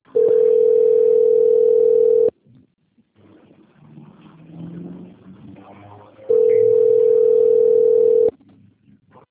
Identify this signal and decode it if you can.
Ringback tone (North America), 440 + 480 Hz.